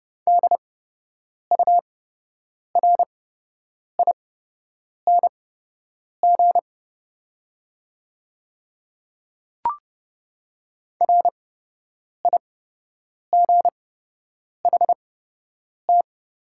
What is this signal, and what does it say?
Morse code, heard as a keyed tone, about 30 wpm, spaced as for 8 wpm, 700 Hz. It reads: DURING RIGHT
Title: 8x Character Spacing - Top 500 Words - 30wpm